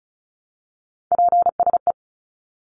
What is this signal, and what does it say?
Morse code, heard as a keyed tone, about 34 wpm, 700 Hz. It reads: PSE
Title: Abbreviations Repeated 3x - 35wpm